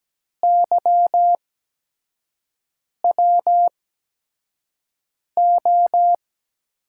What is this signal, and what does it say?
Morse code, heard as a keyed tone, about 17 wpm, spaced as for 5 wpm, 700 Hz. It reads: YWO
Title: USA Call Signs - 6 Characters - 8x Character Spacing - 17wpm